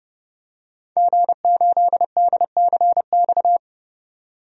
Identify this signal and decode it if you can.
Morse code, heard as a keyed tone, about 31 wpm, 700 Hz. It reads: G8DCX
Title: DXCC-100 Call Signs - 5 Characters - 30wpm